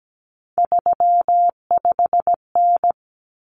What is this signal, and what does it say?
Morse code, heard as a keyed tone, about 17 wpm, 700 Hz. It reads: 35N
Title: Common Cut Numbers - 3-Digit 17wpm